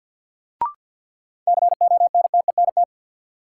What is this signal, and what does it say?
Morse code, heard as a keyed tone, about 50 wpm, 700 Hz. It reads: CONTENT
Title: Sentences from Top 900 Words - New Word Drill - 50wpm